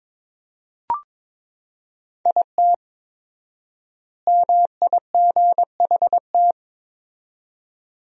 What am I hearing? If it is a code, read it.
Morse code, heard as a keyed tone, about 22 wpm, 700 Hz. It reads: IT MIGHT
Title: 4x Word-Spacing - Sentences from Top 500 Words - 22wpm